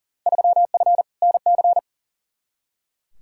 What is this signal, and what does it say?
Morse code, heard as a keyed tone, about 38 wpm, 700 Hz. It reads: 3F NC
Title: Field Day Exchanges - 40wpm